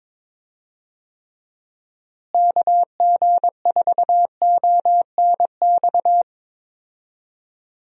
Morse code, heard as a keyed tone, about 22 wpm, 700 Hz. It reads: KG4ONX
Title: USA Call Signs - 6 Characters - Repeated 2x - 22wpm